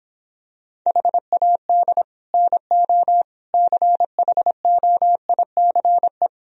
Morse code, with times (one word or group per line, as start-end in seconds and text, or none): 0.9-2.0 HAD
2.3-3.2 NO
3.5-6.3 CHOICE